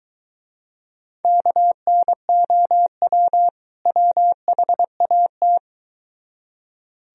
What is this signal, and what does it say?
Morse code, heard as a keyed tone, about 23 wpm, 700 Hz. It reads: KNOW WHAT